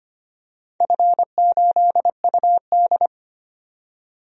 Morse code, heard as a keyed tone, about 25 wpm, 700 Hz. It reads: F8UD
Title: DXCC-100 Call Signs - 4 Characters - 25wpm